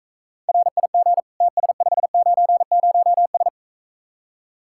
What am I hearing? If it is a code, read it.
Morse code, heard as a keyed tone, about 43 wpm, 700 Hz. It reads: RIG TS590S